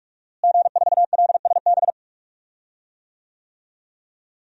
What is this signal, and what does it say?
Morse code, heard as a keyed tone, about 46 wpm, 700 Hz. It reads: G4LSB
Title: DXCC-100 Call Signs - 5 Characters - Repeated 3x - 45wpm